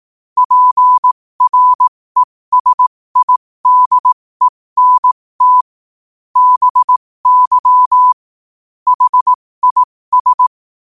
Morse code, heard as a keyed tone, about 18 wpm, spaced as for 15 wpm, 1000 Hz. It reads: PRESIDENT BY HIS
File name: qotdfiles / 15wpm